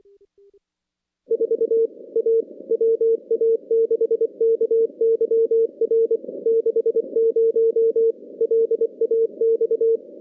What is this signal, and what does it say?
Morse code, heard as a keyed tone, about 24 wpm, 450 Hz. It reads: 4 A WA6KYR 60 LAX
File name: contests / CW-SS-k5tr- -2100